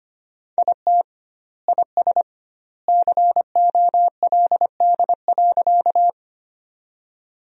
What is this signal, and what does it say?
Morse code, heard as a keyed tone, about 25 wpm, 700 Hz. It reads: IT IS COLD.
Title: 2x Word-Spacing - Sentences from Top 400 Words - 25wpm